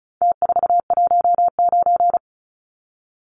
Morse code, heard as a keyed tone, about 34 wpm, 700 Hz. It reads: T419
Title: Numbers - Contest Style - 4-Digit 35wpm